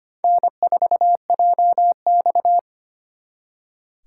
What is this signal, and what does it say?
Morse code, heard as a keyed tone, about 32 wpm, 700 Hz. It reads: N4JX